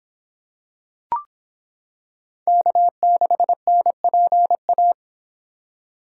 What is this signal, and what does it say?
Morse code, heard as a keyed tone, about 26 wpm, 700 Hz. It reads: K6NPA